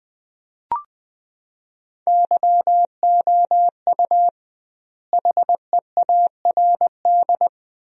Morse code, heard as a keyed tone, about 20 wpm, 700 Hz. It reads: YOU HEARD
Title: 2x Word-Spacing - Sentences from Top 500 Words - 20wpm